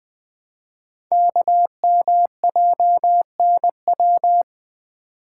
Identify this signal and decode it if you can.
Morse code, heard as a keyed tone, about 20 wpm, 700 Hz. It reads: KMJNW
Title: Code Groups - 5 Letters - 20wpm